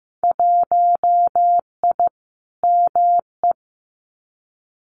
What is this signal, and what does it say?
Morse code, heard as a keyed tone, about 15 wpm, 700 Hz. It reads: 1I ME
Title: Winter Field Day Exchanges Encore 15wpm